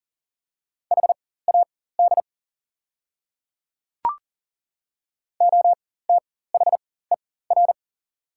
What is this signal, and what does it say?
Morse code, heard as a keyed tone, about 40 wpm, spaced as for 19 wpm, 690 Hz. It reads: HAD OTHER
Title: 4x Character Spacing - Top 100 Words - 40wpm